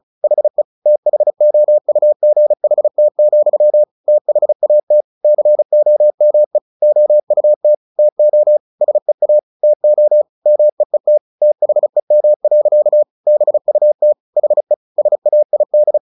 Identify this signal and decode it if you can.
Morse code, heard as a keyed tone, about 34 wpm, 600 Hz. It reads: HE THOUGHT, THAT COME OUT TO SEA TO MEET THEM. BUT HE SAID
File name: the old man and the sea / the old man and the sea 1 35wpm